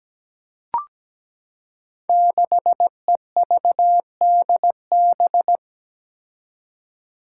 Morse code, heard as a keyed tone, about 17 wpm, 700 Hz. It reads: 6EVDB
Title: Five Challenging Letter-Number Combinations 17wpm